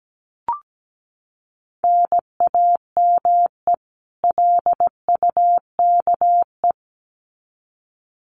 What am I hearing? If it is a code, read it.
Morse code, heard as a keyed tone, about 17 wpm, 700 Hz. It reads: NAME LUKE
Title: Common QSO Phrases 17wpm